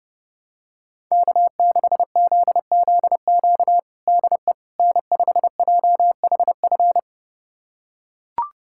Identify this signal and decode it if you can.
Morse code, heard as a keyed tone, about 30 wpm, 700 Hz. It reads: K6ZZQ DE N5JHF
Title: USA Callsign DE Callsign - 5 Characters - 30wpm